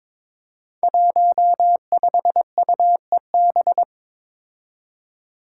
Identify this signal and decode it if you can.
Morse code, heard as a keyed tone, about 21 wpm, 700 Hz. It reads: 15UEB
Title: Five Challenging Letter-Number Combinations 22wpm